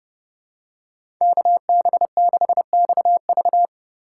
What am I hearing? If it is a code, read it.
Morse code, heard as a keyed tone, about 30 wpm, 700 Hz. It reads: KB6XV
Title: USA Call Signs - 5 Characters - 2x Character Spacing - 30wpm